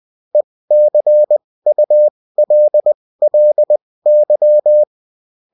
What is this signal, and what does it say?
Morse code, heard as a keyed tone, about 20 wpm, spaced as for 16 wpm, 600 Hz. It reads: ECULLY
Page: bulletin-uft-sous-forme-texte-et-en-cw